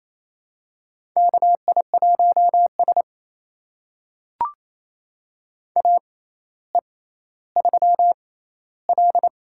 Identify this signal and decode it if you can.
Morse code, heard as a keyed tone, about 28 wpm, spaced as for 14 wpm, 700 Hz. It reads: KI1S AE3L
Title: USA Call Signs - 4 Characters - 6x Character Spacing - 28wpm